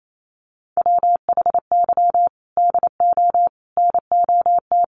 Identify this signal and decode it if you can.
Morse code, heard as a keyed tone, about 28 wpm, 700 Hz. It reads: WHY DO NOT
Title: Sets of 5 Words Encore 28wpm